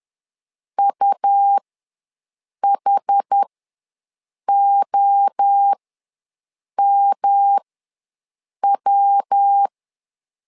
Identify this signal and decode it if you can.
Morse code, heard as a keyed tone, about 10 wpm, spaced as for 6 wpm, 800 Hz. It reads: UHOMW
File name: LEZIONI / 3 MWRO / 3 MWRO ws 10wpm 2310